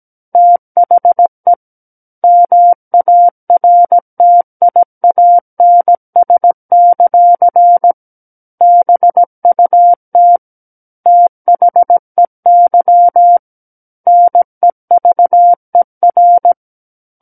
Morse code, heard as a keyed tone, about 17 wpm, 700 Hz. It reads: THE MARTIANS; BUT THEY NEVER
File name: Book One / 08-Chapter 8 - 17 WPM